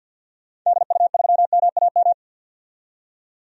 Morse code, heard as a keyed tone, about 49 wpm, 700 Hz. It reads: DU3MAM
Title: DXCC-100 Call Signs - 6 Characters - 50wpm